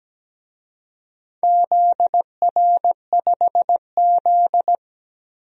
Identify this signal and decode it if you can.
Morse code, heard as a keyed tone, about 17 wpm, 700 Hz. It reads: ZR5Z
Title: DXCC-100 Call Signs - 4 Characters - 17wpm